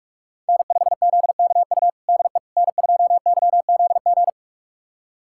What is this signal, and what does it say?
Morse code, heard as a keyed tone, about 41 wpm, 700 Hz. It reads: N5ZKU DE N2YZG
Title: USA Callsign DE Callsign - 5 Characters - 45wpm